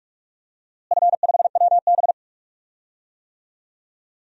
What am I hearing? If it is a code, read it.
Morse code, heard as a keyed tone, about 44 wpm, 700 Hz. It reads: F5WB
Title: DXCC-100 Call Signs - 4 Characters - Repeated 2x - 45wpm